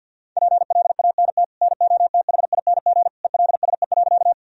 Morse code, heard as a keyed tone, about 51 wpm, 700 Hz. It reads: PLANT NOTHING ELSE.
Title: Sentences from Top 1000 Words - Review Part 1 - 50wpm